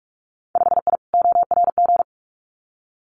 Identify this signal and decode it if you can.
Morse code, heard as a keyed tone, about 46 wpm, 700 Hz. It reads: 5I ORG
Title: Winter Field Day Exchanges Encore 45wpm